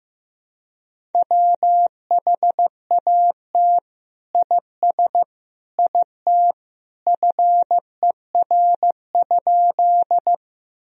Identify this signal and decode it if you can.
Morse code, heard as a keyed tone, about 16 wpm, 700 Hz. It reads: WHAT IS IT FER?
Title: Sentences from Top 100 Words and Abbreviations 15wpm